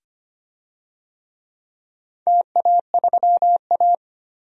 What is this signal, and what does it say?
Morse code, heard as a keyed tone, about 25 wpm, 700 Hz. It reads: TA3A
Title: DXCC-100 Call Signs - 4 Characters - 25wpm